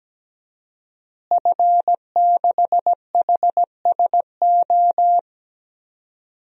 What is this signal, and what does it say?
Morse code, heard as a keyed tone, about 17 wpm, 700 Hz. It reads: F6HSO
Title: DXCC-100 Call Signs - 5 Characters - Repeated 3x - 17wpm